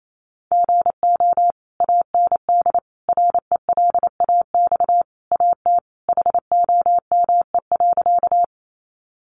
Morse code, heard as a keyed tone, about 28 wpm, 700 Hz. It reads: GO AND RELAX AT HOME.
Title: Sentences from Top 1400 Words 28wpm